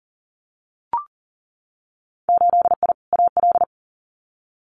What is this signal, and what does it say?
Morse code, heard as a keyed tone, about 37 wpm, 700 Hz. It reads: 8I AL
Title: Winter Field Day Exchanges Encore 40wpm